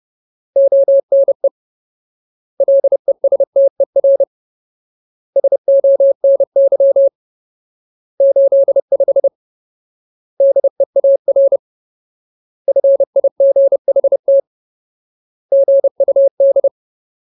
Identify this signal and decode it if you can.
Morse code, heard as a keyed tone, about 30 wpm, 550 Hz. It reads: ONE LESTER SONY 85 DEAR FIGHT GUD
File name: Ultimate Word List / 30WPM Ultimate